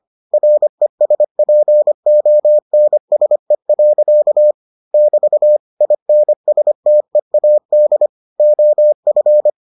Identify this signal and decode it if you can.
Morse code, heard as a keyed tone, about 25 wpm, 600 Hz. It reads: RESPONSE. <BT> INSTEAD OF